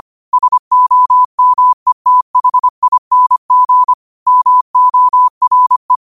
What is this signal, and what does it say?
Morse code, heard as a keyed tone, about 25 wpm, 1000 Hz. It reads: SOMETHING MORE
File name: qotdfiles / 25wpm